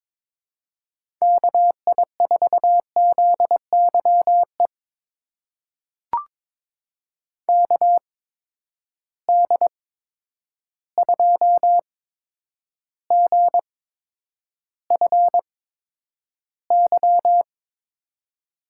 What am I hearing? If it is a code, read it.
Morse code, heard as a keyed tone, about 22 wpm, spaced as for 9 wpm, 700 Hz. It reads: KI4ZYE KD2GFY
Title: USA Call Signs - 8x Character Spacing - 22wpm